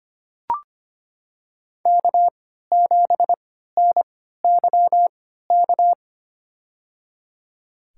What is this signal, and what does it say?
Morse code, heard as a keyed tone, about 25 wpm, spaced as for 14 wpm, 700 Hz. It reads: K7NYK
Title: USA Call Signs - 3x Character Spacing - 25wpm